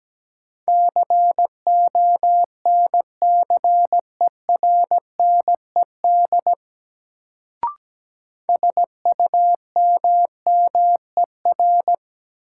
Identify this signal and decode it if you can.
Morse code, heard as a keyed tone, about 17 wpm, 700 Hz. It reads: CONCERNED SUMMER
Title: Sentences from Top 1000 Words - Review Part 6 - New Word Drill - 17wpm